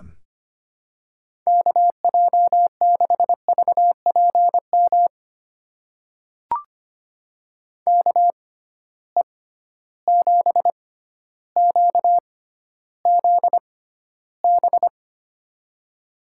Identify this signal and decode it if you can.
Morse code, heard as a keyed tone, about 25 wpm, spaced as for 13 wpm, 700 Hz. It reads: KJ6VPM KE7QZB
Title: USA Call Signs - 6 Characters - 6x Character Spacing - 25wpm